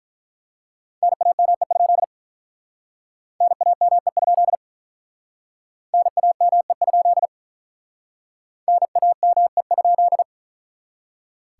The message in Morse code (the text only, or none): NAME? NAME? NAME? NAME?